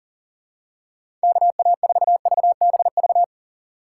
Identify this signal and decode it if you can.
Morse code, heard as a keyed tone, about 40 wpm, 700 Hz. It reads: KA4VBV
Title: USA Call Signs - 6 Characters - 3x Character Spacing - 40wpm